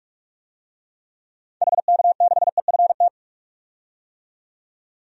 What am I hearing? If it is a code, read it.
Morse code, heard as a keyed tone, about 42 wpm, 700 Hz. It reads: HK6EFT
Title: DXCC-100 Call Signs - 6 Characters - 45wpm